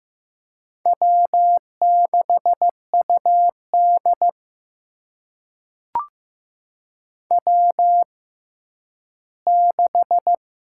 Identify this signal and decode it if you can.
Morse code, heard as a keyed tone, about 15 wpm, spaced as for 10 wpm, 700 Hz. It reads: W6UD W6
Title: USA Call Signs - 6x Character Spacing - 15wpm